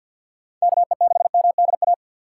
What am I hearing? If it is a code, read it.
Morse code, heard as a keyed tone, about 49 wpm, 700 Hz. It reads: KE6MDA
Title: USA Call Signs - 6 Characters - 3x Character Spacing - 50wpm